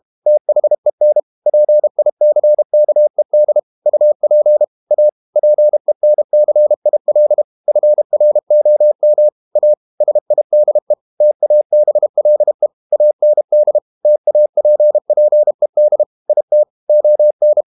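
Morse code, heard as a keyed tone, about 32 wpm, 600 Hz. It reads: THEN PICKED UP A PENCIL FROM A SIDE TABLE AND TAPPED IT ON
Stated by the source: Morse code practice at 32 wpm from CW med Gnister, reading Ken Follett's novel Never in English.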